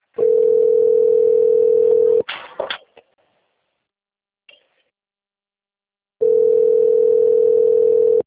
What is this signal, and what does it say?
Ringback tone (North America), 440 + 480 Hz.